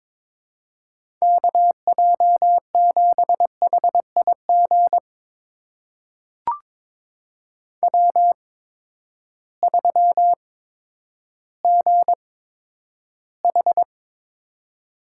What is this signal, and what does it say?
Morse code, heard as a keyed tone, about 21 wpm, spaced as for 11 wpm, 700 Hz. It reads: KJ7HIG W3GH
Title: USA Call Signs - 8x Character Spacing - 22wpm